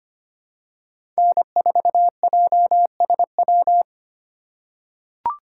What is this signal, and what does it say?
Morse code, heard as a keyed tone, about 25 wpm, 700 Hz. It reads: N4JSW
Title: USA Call Signs - 5 Characters - 3x Character Spacing - 25wpm